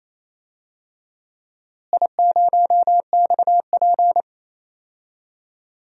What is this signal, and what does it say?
Morse code, heard as a keyed tone, about 28 wpm, 700 Hz. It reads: I0XP
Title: DXCC-100 Call Signs - 4 Characters - Repeated 3x - 28wpm